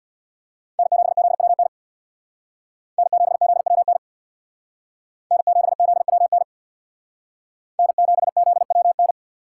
Morse code, heard as a keyed tone, about 64 wpm, 700 Hz. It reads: N7ZWN N7ZWN N7ZWN N7ZWN